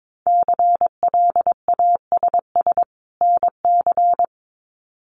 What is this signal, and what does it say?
Morse code, heard as a keyed tone, about 22 wpm, 700 Hz. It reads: CLASS NC
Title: Winter Field Day Exchanges 22wpm